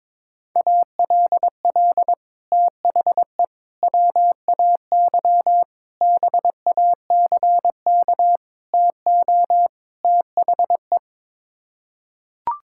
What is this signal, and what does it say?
Morse code, heard as a keyed tone, about 22 wpm, 700 Hz. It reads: ALL THE WAY BACK TO THE
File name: practice / 22wpm / Sets of 6 Words - 22wpm